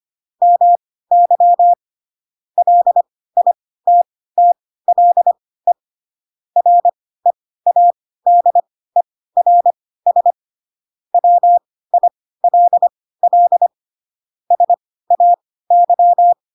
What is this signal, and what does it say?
Morse code, heard as a keyed tone, about 25 wpm, spaced as for 16 wpm, 700 Hz. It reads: MY LITTLE READERS WILL SAY
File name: Pinocchio / Pinocchio ch1 10 25 ik1hge